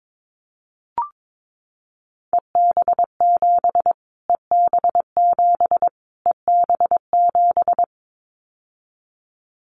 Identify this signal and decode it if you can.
Morse code, heard as a keyed tone, about 22 wpm, 700 Hz. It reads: EB7 EB7 EB7
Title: Callsign Prefixes for Top 25 DXCC - Repeated 3x - 22wpm